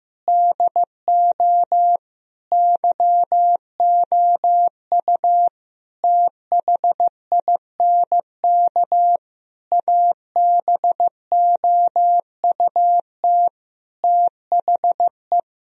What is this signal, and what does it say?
Morse code, heard as a keyed tone, about 15 wpm, 700 Hz. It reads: DO YOU THINK ABOUT THE
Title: Sentences from Top 1000 Words - Review Part 3 - 15wpm